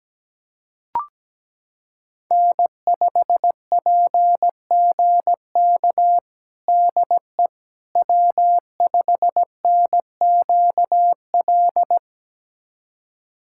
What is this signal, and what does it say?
Morse code, heard as a keyed tone, about 17 wpm, 700 Hz. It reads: N5PGK DE W5NQL
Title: USA Callsign DE Callsign - 5 Characters - 17wpm